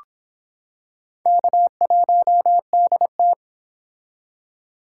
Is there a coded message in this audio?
Yes: Morse code from 1.3 to 3.3 s, heard as a keyed tone, about 26 wpm, 700 Hz. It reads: K1DT